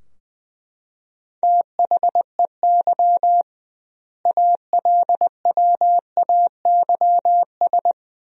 Morse code, heard as a keyed tone, about 20 wpm, 700 Hz. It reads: THEY ALWAYS
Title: 2x Word-Spacing - Sentences from Top 300 Words - 20wpm